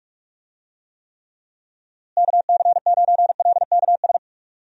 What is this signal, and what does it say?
Morse code, heard as a keyed tone, about 46 wpm, 700 Hz. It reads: KC9LKS
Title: USA Call Signs - 6 Characters - Repeated 3x - 45wpm